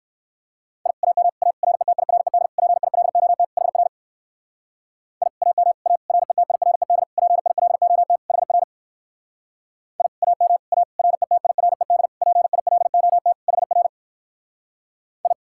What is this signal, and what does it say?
Morse code, heard as a keyed tone, about 59 wpm, 690 Hz. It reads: I AM A RETIRED PILOT HR I AM A RETIRED PILOT HR I AM A RETIRED PILOT HR I